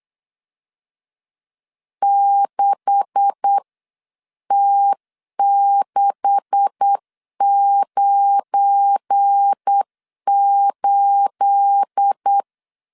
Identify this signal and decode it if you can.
Morse code, heard as a keyed tone, about 8 wpm, spaced as for 7 wpm, 800 Hz. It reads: 6T698
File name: LEZIONI / 9 286 8wpm 3425